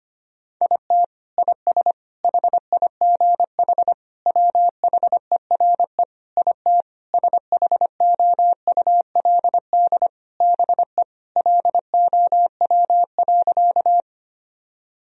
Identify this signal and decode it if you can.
Morse code, heard as a keyed tone, about 25 wpm, 700 Hz. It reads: IT IS HIGH WHERE IT SHOULD BE LOW.
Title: Sentences from Top 1000 Words - Review Part 1 - 25wpm